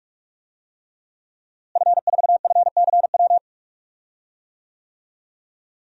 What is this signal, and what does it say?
Morse code, heard as a keyed tone, about 44 wpm, 700 Hz. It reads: F4FCW
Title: DXCC-100 Call Signs - 5 Characters - Repeated 3x - 45wpm